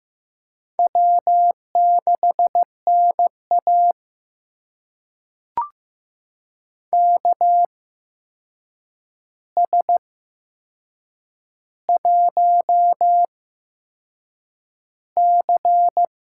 Morse code, heard as a keyed tone, about 15 wpm, spaced as for 6 wpm, 700 Hz. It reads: W6NA KS1C
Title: USA Call Signs - 4 Characters - 8x Character Spacing - 15wpm